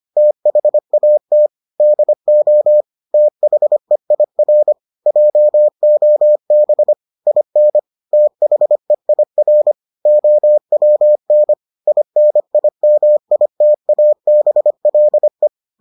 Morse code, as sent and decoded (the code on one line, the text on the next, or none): - .... .- - / -.. --- / - .... .. .-. / .--- --- -... / .. -. / - .... .. .-. / --- .-- -. / .. -. .. -- .. - .- -... .-.. .
THAT DO THEIR JOB IN THEIR OWN INIMITABLE